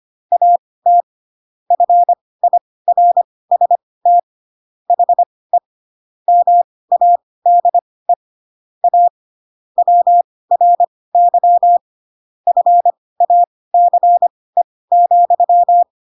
Morse code, heard as a keyed tone, about 25 wpm, spaced as for 18 wpm, 700 Hz. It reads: AT FIRST HE MADE A WRY FACE,
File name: Pinocchio / Pinocchio ch7 12 25 ik1hge